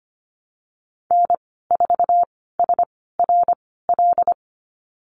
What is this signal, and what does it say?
Morse code, heard as a keyed tone, about 25 wpm, spaced as for 16 wpm, 700 Hz. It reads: N4SRL
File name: Lesson 15g - Number 5 - Call Signs 25wpm